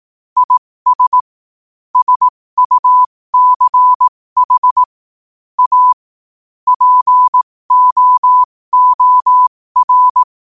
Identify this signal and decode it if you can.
Morse code, heard as a keyed tone, about 18 wpm, spaced as for 15 wpm, 1000 Hz. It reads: IS SUCH A POOR
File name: qotdfiles / 15wpm